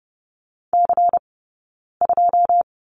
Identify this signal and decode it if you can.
Morse code, heard as a keyed tone, about 30 wpm, spaced as for 9 wpm, 700 Hz. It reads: C2
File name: Lesson 20d - Number 3 - Two Characters 30wpm